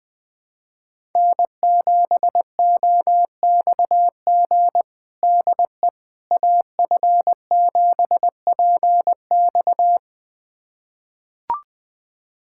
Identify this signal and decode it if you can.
Morse code, heard as a keyed tone, about 20 wpm, 700 Hz. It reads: N7OXG DE AF7PX
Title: USA Callsign DE Callsign - 5 Characters - 20wpm